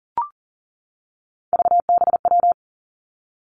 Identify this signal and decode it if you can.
Morse code, heard as a keyed tone, about 40 wpm, 700 Hz. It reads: VBW
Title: Three Challenging Letter-Number Combinations 40wpm